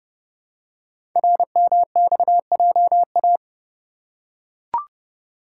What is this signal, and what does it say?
Morse code, heard as a keyed tone, about 30 wpm, 700 Hz. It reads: RMXJA
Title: Code Groups - 5 Letters - 30wpm